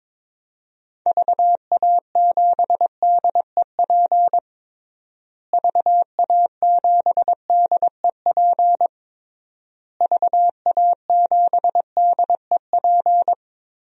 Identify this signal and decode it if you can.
Morse code, heard as a keyed tone, about 22 wpm, 700 Hz. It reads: VA7DEP VA7DEP VA7DEP
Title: DXCC-100 Call Signs - 6 Characters - Repeated 3x - 22wpm